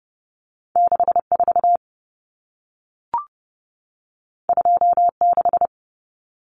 Morse code, heard as a keyed tone, about 30 wpm, 700 Hz. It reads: 64 26
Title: Common Cut Numbers - 2-Digit 30wpm